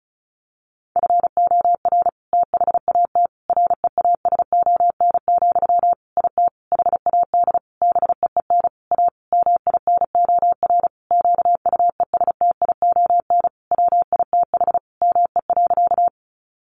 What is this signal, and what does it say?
Morse code, heard as a keyed tone, about 34 wpm, 700 Hz. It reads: FOR THAT REASON, IT HAD BEEN A MINOR QUESTION WITH ME.
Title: Sentences from Top 1500 Words 35wpm